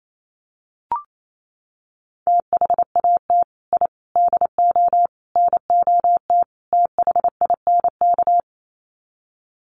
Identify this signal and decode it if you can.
Morse code, heard as a keyed tone, about 28 wpm, 700 Hz. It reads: THAT I DO NOT THINK
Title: Sets of 5 Words Encore 28wpm